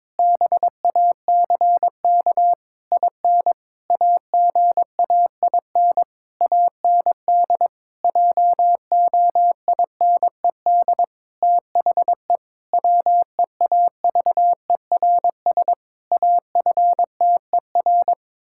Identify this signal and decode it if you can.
Morse code, heard as a keyed tone, about 22 wpm, 700 Hz. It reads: BACK IN AGAIN AND JOINED THE WEAVERS AFTER
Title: Pangram Challenge v2 - 22wpm